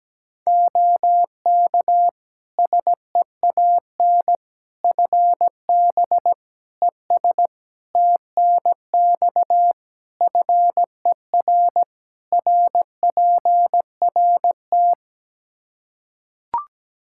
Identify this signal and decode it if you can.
Morse code, heard as a keyed tone, about 17 wpm, 700 Hz. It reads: OK SEAN FB ES TNX FER RPRT
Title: Practice QSO - Exchange 2 - Info - 17wpm